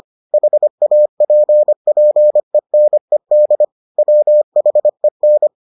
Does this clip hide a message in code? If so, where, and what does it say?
Yes: Morse code from 0.3 to 5.5 s, heard as a keyed tone, about 25 wpm, 600 Hz. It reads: HAPPENED WHEN